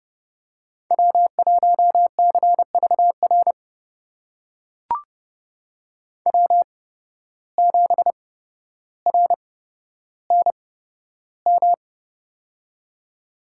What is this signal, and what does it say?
Morse code, heard as a keyed tone, about 30 wpm, spaced as for 13 wpm, 700 Hz. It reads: W1CVR W7RNM
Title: USA Call Signs - 5 Characters - 8x Character Spacing - 30wpm